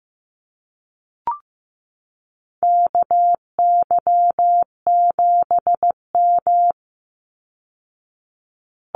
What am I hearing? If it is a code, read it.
Morse code, heard as a keyed tone, about 15 wpm, 700 Hz. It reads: KY7M
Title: Call Signs Repeated 3x - 15wpm